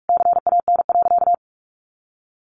Morse code, heard as a keyed tone, about 44 wpm, 700 Hz. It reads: CAN.
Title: Sentences from Top 1100 Words 45wpm